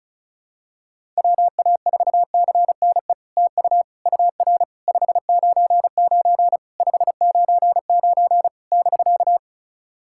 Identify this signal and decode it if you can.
Morse code, heard as a keyed tone, about 36 wpm, 700 Hz. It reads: WA4CNE TU UR 599 599 <BK>
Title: SOTA QSO Practice 35wpm